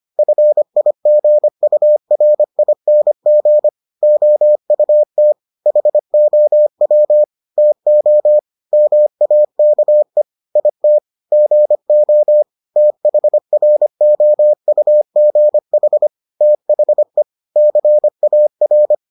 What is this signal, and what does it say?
Morse code, heard as a keyed tone, about 25 wpm, 600 Hz. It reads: FIGURING OUT HOW TO MAKE IT GO THROUGH THE CAR